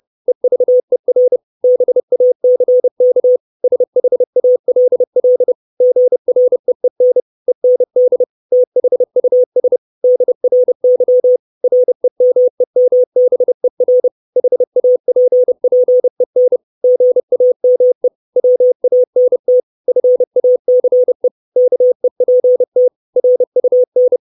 Morse code, heard as a keyed tone, about 30 wpm, 500 Hz. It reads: EVER BACK SHALL GREEN END THUS DRY REMEMBER HAPPEN GAME WANT FACE KEPT RUN
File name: 632 words / 632 Words 30WPM